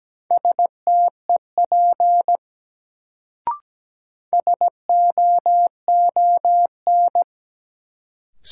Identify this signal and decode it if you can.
Morse code, heard as a keyed tone, about 17 wpm, 700 Hz. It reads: STEP SOON
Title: Sentences from Top 400 Words - New Word Drill - 17wpm